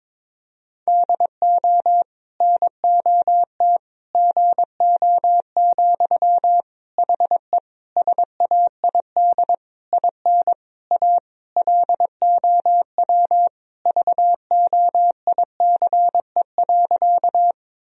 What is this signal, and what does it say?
Morse code, heard as a keyed tone, about 21 wpm, 700 Hz. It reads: DO NOT GO, HE SAID IN A LOW VOICE.
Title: Sentences from Top 1000 Words - Review Part 1 - 22wpm